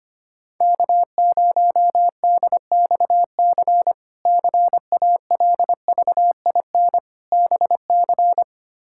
Morse code, heard as a keyed tone, about 25 wpm, 700 Hz. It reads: K0DXC CALVIN BC
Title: K1USN SST QSO Practice - 25wpm